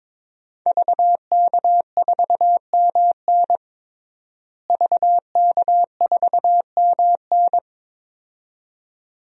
Morse code, heard as a keyed tone, about 22 wpm, 700 Hz. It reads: VK4MN VK4MN
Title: DXCC-100 Call Signs - 5 Characters - Repeated 2x - 22wpm